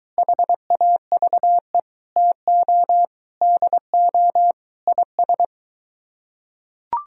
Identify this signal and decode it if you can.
Morse code, heard as a keyed tone, about 23 wpm, 700 Hz. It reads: HAVE TO DO IS